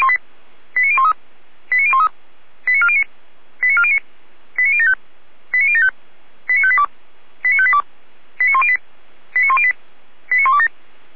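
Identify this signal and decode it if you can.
Five-tone selective call (ZVEI), heard as quick runs of five tones, tones of about 70 ms. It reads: 89013 89013 89409 89409 89075 89075 89582 89582 89108 89108 89127
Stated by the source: ZVEI five-tone call received on SDR.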